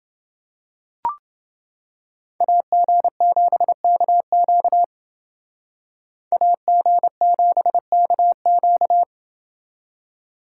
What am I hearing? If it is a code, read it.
Morse code, heard as a keyed tone, about 29 wpm, 700 Hz. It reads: AG7KQ AG7KQ